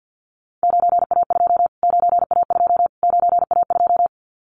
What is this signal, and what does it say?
Morse code, heard as a keyed tone, about 50 wpm, 700 Hz. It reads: 9A2 9A2 9A2
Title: Callsign Prefixes for Top 25 DXCC - Repeated 3x - 50wpm